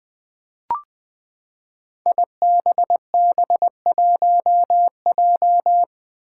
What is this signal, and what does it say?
Morse code, heard as a keyed tone, about 20 wpm, 700 Hz. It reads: IBB1J
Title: Five Challenging Letter-Number Combinations 20wpm